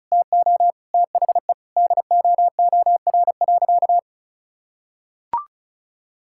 Morse code, heard as a keyed tone, about 34 wpm, 700 Hz. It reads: TO THE DOOR.
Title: Sentences from Top 1600 Words 35wpm